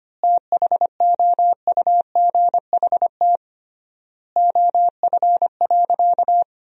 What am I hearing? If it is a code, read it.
Morse code, heard as a keyed tone, about 25 wpm, 700 Hz. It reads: THOUGHT OF.
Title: 3x Word-Spacing - Sentences from Top 300 Words - 25wpm